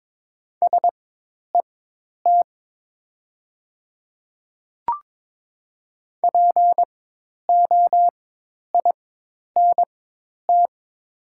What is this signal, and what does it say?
Morse code, heard as a keyed tone, about 22 wpm, spaced as for 10 wpm, 700 Hz. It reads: SET POINT
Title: 4x Character Spacing - Top 300 Words - 22wpm